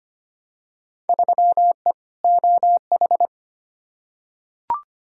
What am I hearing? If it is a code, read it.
Morse code, heard as a keyed tone, about 25 wpm, 700 Hz. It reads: 3E OH
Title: Field Day Exchanges - 25wpm